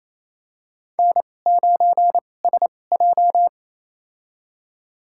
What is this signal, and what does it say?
Morse code, heard as a keyed tone, about 28 wpm, spaced as for 20 wpm, 700 Hz. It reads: N9SJ
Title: USA Call Signs - 4 Characters - 2x Character Spacing - 28wpm